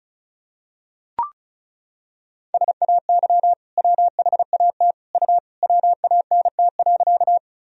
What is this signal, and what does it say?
Morse code, heard as a keyed tone, about 36 wpm, 700 Hz. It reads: SAY WHAT U WANT.
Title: Sentences from Top 200 Words and Abbreviations 35wpm